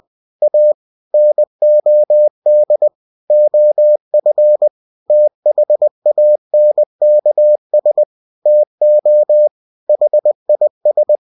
Morse code, heard as a keyed tone, about 20 wpm, 600 Hz. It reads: A NOD OF THANKS TO HIS
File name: cw-books / winnie-the-pooh / winnie-20wpm-page-0020